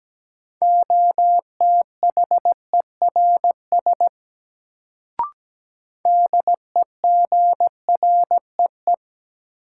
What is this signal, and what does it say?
Morse code, heard as a keyed tone, about 17 wpm, 700 Hz. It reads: OTHERS DEGREE
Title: Sentences from Top 1500 Words - New Word Drill Alt - 17wpm